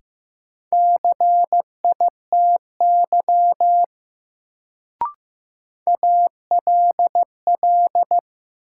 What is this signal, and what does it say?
Morse code, heard as a keyed tone, about 15 wpm, 700 Hz. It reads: CITY ALL
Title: Sets of 2 Words - New Word Drill - 15wpm